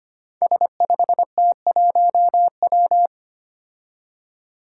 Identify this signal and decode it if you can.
Morse code, heard as a keyed tone, about 25 wpm, 700 Hz. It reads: S5T1W
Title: Five Challenging Letter-Number Combinations 25wpm